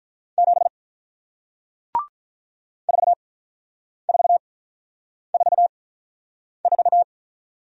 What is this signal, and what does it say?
Morse code, heard as a keyed tone, about 48 wpm, 700 Hz. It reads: 7 4 4 4 4